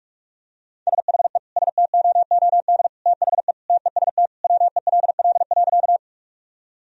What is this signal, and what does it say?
Morse code, heard as a keyed tone, about 45 wpm, 700 Hz. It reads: SHE STOOD THE TEST WELL.
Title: Sentences from Top 1000 Words - Review Part 3 - 45wpm